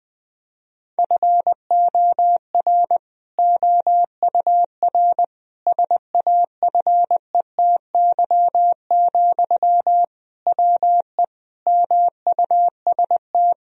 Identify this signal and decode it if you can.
Morse code, heard as a keyed tone, about 20 wpm, 700 Hz. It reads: FOR OUR SAFETY, WE MUST
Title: Sentences from Top 1000 Words - Review Part 4 - 20wpm